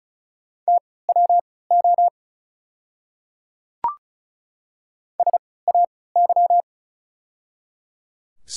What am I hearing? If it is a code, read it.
Morse code, heard as a keyed tone, about 36 wpm, spaced as for 20 wpm, 700 Hz. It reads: TWO SAY
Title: 3x Character Spacing - Top 200 Words - 35wpm